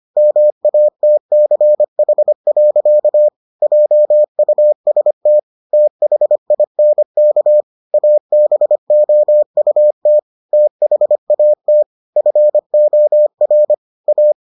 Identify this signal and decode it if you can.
Morse code, heard as a keyed tone, about 25 wpm, 600 Hz. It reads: MATCH. JUST THINK ABOUT THAT FOR A